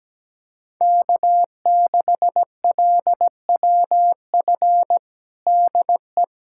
Morse code, heard as a keyed tone, about 17 wpm, 700 Hz. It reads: K6LWF DE
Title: USA Callsign DE Callsign - 5 Characters - 17wpm